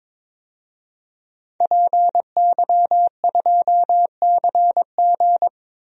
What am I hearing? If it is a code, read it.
Morse code, heard as a keyed tone, about 22 wpm, 700 Hz. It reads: PY2CG
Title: DXCC-100 Call Signs - 5 Characters - 22wpm